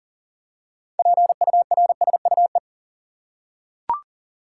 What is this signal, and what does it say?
Morse code, heard as a keyed tone, about 40 wpm, 700 Hz. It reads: PURSUE
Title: Six Letter Words - 40wpm